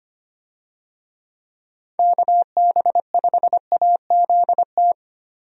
Morse code, heard as a keyed tone, about 25 wpm, 700 Hz. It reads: KB5AZT
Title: USA Call Signs - 6 Characters - Repeated 3x - 25wpm